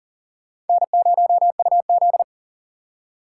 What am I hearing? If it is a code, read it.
Morse code, heard as a keyed tone, about 50 wpm, 700 Hz. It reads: N0UZ
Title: Speed Racing - USA Call Signs 40wpm